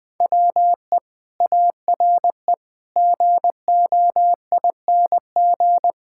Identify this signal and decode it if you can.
Morse code, heard as a keyed tone, about 20 wpm, 700 Hz. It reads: WE ARE GOING
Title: Sets of 6 Words - 20wpm